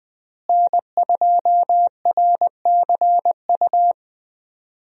Morse code, heard as a keyed tone, about 20 wpm, 700 Hz. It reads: N2RCU
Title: USA Call Signs - 5 Characters - Repeated 2x - 20wpm